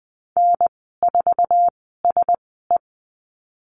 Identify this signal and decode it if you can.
Morse code, heard as a keyed tone, about 20 wpm, spaced as for 15 wpm, 700 Hz. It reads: N4SE
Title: 055 Lesson 10g - Letter r - Call Signs 20wpm